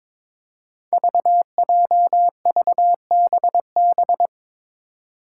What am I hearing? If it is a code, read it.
Morse code, heard as a keyed tone, about 22 wpm, 700 Hz. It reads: VJVBB
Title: Five Challenging Letter-Number Combinations 22wpm